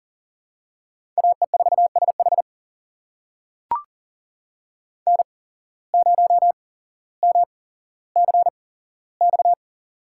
Morse code, heard as a keyed tone, about 40 wpm, spaced as for 17 wpm, 700 Hz. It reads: AE4SH N0MCX
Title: USA Call Signs - 5 Characters - 8x Character Spacing - 40wpm